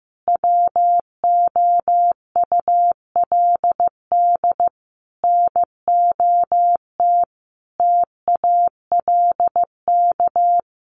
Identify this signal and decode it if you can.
Morse code, heard as a keyed tone, about 15 wpm, 700 Hz. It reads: WOULD NOT TALK
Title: Sentences from Top 800 Words 15wpm